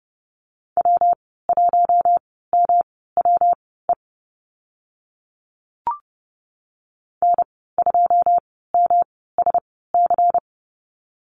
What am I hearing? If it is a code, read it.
Morse code, heard as a keyed tone, about 30 wpm, spaced as for 17 wpm, 700 Hz. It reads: W1MWE N2MSC
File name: Lesson 19g - Letter w - Call Signs 30wpm